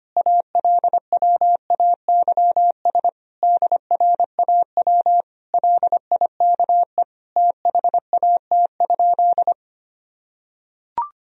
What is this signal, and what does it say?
Morse code, heard as a keyed tone, about 25 wpm, 700 Hz. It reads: ALWAYS DRAW LIKE THAT?
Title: Sentences from Top 1000 Words - Review Part 1 - 25wpm